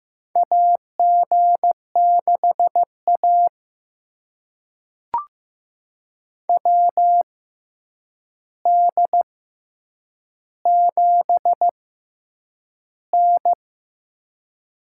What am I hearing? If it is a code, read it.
Morse code, heard as a keyed tone, about 15 wpm, spaced as for 8 wpm, 700 Hz. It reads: AG6A WD7N
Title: USA Call Signs - 6x Character Spacing - 15wpm